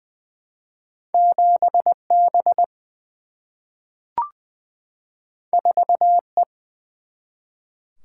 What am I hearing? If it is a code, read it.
Morse code, heard as a keyed tone, about 20 wpm, 700 Hz. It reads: 7 B 4 E
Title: Single-Letter Single-Number 20wpm